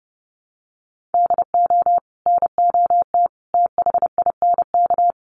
Morse code, heard as a keyed tone, about 30 wpm, 700 Hz. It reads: DO NOT THINK
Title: Sets of 3 Words 30wpm